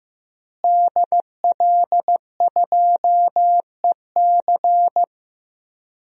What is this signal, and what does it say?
Morse code, heard as a keyed tone, about 15 wpm, 700 Hz. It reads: DL2EC
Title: DXCC-100 Call Signs - 5 Characters - Repeated 3x - 15wpm